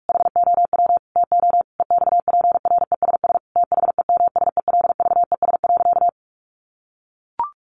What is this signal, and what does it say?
Morse code, heard as a keyed tone, about 44 wpm, 700 Hz. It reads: HOW TO EXPRESS THEMSELVES.
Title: Sentences from Top 1100 Words 45wpm